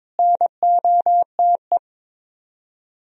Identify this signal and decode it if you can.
Morse code, heard as a keyed tone, about 24 wpm, 700 Hz. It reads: NOTE
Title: Speed-Racing - Sets of 5 Words 22wpm